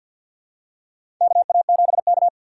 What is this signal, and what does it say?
Morse code, heard as a keyed tone, about 51 wpm, 700 Hz. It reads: KA7K
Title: USA Call Signs - 4 Characters - 2x Character Spacing - 50wpm